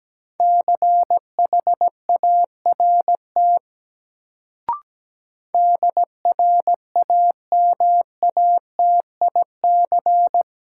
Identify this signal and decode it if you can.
Morse code, heard as a keyed tone, about 17 wpm, 700 Hz. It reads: CHART DRAMATIC
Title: Sentences from Top 1300 Words - New Word Drill Alt - 17wpm